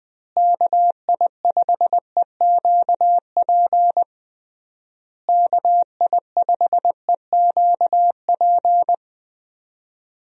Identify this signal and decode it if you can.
Morse code, heard as a keyed tone, about 21 wpm, 700 Hz. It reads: KI5EQP KI5EQP
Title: USA Call Signs - 6 Characters - Repeated 3x - 20wpm